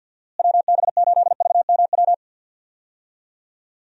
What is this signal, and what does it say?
Morse code, heard as a keyed tone, about 49 wpm, 700 Hz. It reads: WB8VMW
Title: USA Call Signs - 6 Characters - Repeated 2x - 50wpm